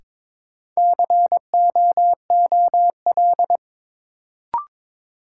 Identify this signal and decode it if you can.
Morse code, heard as a keyed tone, about 22 wpm, 700 Hz. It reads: COOL
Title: Sentences from Top 1200 Words - New Word Drill - 22wpm